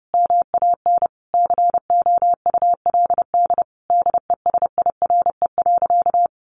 Morse code, heard as a keyed tone, about 30 wpm, 700 Hz. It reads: MAN COULD DESIRE.
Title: Sentences from Top 1100 Words 30wpm